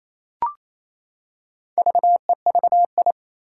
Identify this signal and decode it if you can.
Morse code, heard as a keyed tone, about 28 wpm, 700 Hz. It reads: VEVI
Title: Four Challenging Letter-Number Combinations 28wpm